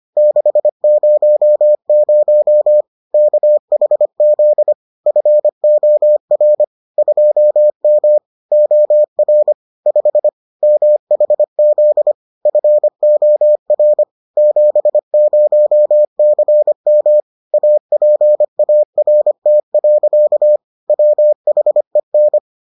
Morse code, heard as a keyed tone, about 25 wpm, 600 Hz. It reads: 600 KHZ FOR 2M OR 5 MHZ FOR 70CM APART. WHEN